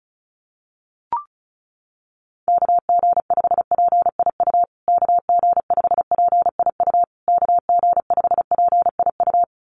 Morse code, heard as a keyed tone, about 34 wpm, 700 Hz. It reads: KG5PIU KG5PIU KG5PIU
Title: USA Call Signs Repeated 3x 35wpm